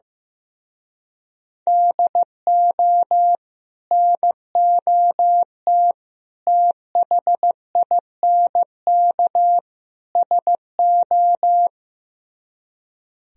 Morse code, heard as a keyed tone, about 15 wpm, 700 Hz. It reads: DO NOT THINK SO
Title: Speed-Racing - Sets of 4 Words 15wpm